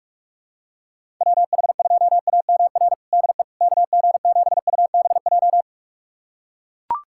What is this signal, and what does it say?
Morse code, heard as a keyed tone, about 44 wpm, 700 Hz. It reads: WH2AMR DE KG7UBJ
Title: USA Callsign DE Callsign - 6 Characters - 45wpm